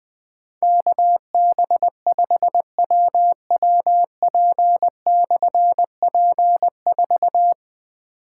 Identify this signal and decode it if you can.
Morse code, heard as a keyed tone, about 20 wpm, 700 Hz. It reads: KB5WWP/P4
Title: Difficult Call Signs 20wpm